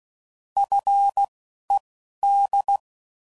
Morse code, heard as a keyed tone, about 16 wpm, spaced as for 11 wpm, 800 Hz. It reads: FED
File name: CW Code Groups @10WPM / Technical Subjects / SwPwrSup